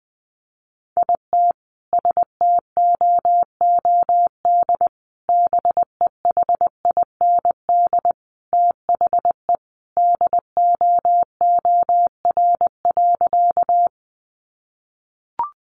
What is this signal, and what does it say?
Morse code, heard as a keyed tone, about 20 wpm, 700 Hz. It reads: IT STOOD BEHIND THE DOOR.
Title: Sentences from Top 500 Words 20wpm